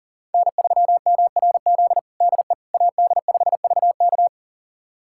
Morse code, heard as a keyed tone, about 37 wpm, 700 Hz. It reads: N3MRZ DE AD5VK
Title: USA Callsign DE Callsign - 5 Characters - 40wpm